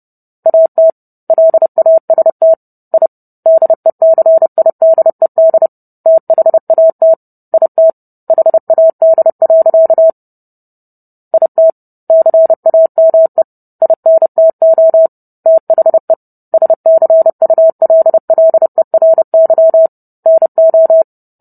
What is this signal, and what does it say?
Morse code, heard as a keyed tone, about 30 wpm, 650 Hz. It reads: AT LAST I DECIDED THAT IT HAD. IT CAME INTO THE SCULLERY NO